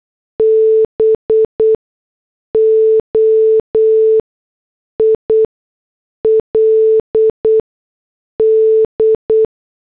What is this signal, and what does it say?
Morse code, heard as a keyed tone, about 8 wpm, spaced as for 6 wpm, 440 Hz. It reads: BOILD
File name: morse / headlines